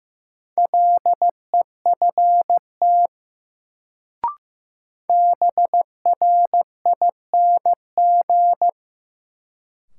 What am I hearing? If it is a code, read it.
Morse code, heard as a keyed tone, about 16 wpm, 700 Hz. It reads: LEFT BRING
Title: Sets of 5 Words Encore - New Word Drill - 15wpm